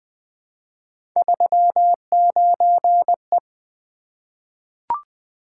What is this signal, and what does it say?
Morse code, heard as a keyed tone, about 20 wpm, 700 Hz. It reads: 39E
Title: Three Letter-Number Combinations - 20wpm